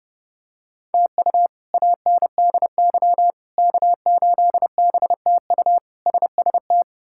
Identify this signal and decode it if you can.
Morse code, heard as a keyed tone, about 30 wpm, 700 Hz. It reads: TU ANDY K8BTU SST
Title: K1USN SST QSO Practice - 30wpm